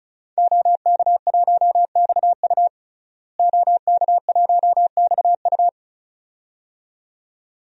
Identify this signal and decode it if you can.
Morse code, heard as a keyed tone, about 34 wpm, 700 Hz. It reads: OK1XU OK1XU
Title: DXCC-100 Call Signs - 5 Characters - Repeated 2x - 35wpm